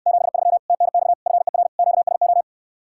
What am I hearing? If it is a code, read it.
Morse code, heard as a keyed tone, about 69 wpm, 700 Hz. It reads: 73 TNX FER QSO